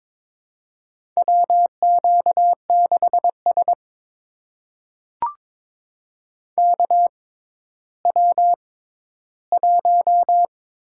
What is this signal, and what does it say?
Morse code, heard as a keyed tone, about 22 wpm, spaced as for 13 wpm, 700 Hz. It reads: WQ6S KW1
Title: USA Call Signs - 4 Characters - 6x Character Spacing - 22wpm